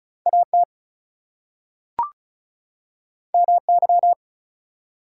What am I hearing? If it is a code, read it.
Morse code, heard as a keyed tone, about 36 wpm, 700 Hz. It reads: AT MY
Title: Two Letter Words - 35wpm